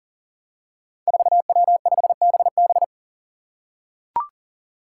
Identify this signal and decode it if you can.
Morse code, heard as a keyed tone, about 40 wpm, 700 Hz. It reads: 4W5BB
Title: Five Challenging Letter-Number Combinations 40wpm